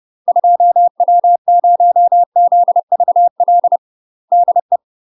Morse code, heard as a keyed tone, about 30 wpm, 700 Hz. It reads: 2W0ZVL DE
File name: UK MASTER QSOs TRAINING / 30WPM / UK MASTER QSOs 0031